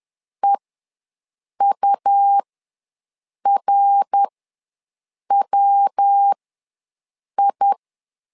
Morse code, heard as a keyed tone, about 11 wpm, spaced as for 6 wpm, 800 Hz. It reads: EURWI
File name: LEZIONI / 3 MWRO / 3 MWR ws 10wpm 2151